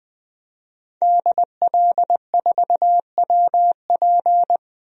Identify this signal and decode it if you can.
Morse code, heard as a keyed tone, about 20 wpm, 700 Hz. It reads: DL4WP
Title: DXCC-100 Call Signs - 5 Characters - Repeated 2x - 20wpm